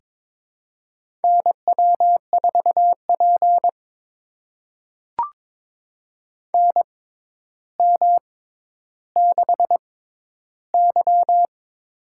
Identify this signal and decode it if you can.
Morse code, heard as a keyed tone, about 22 wpm, spaced as for 11 wpm, 700 Hz. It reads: NW4P NM6Y
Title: USA Call Signs - 4 Characters - 6x Character Spacing - 22wpm